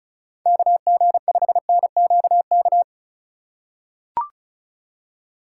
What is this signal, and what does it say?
Morse code, heard as a keyed tone, about 34 wpm, 700 Hz. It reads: KG5NQK
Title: USA Call Signs - 6 Characters - 2x Character Spacing - 35wpm